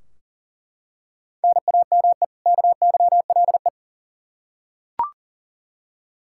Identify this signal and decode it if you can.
Morse code, heard as a keyed tone, about 40 wpm, 700 Hz. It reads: NAME KYLE
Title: Practice QSO - Exchange 1 - Beginning - 40wpm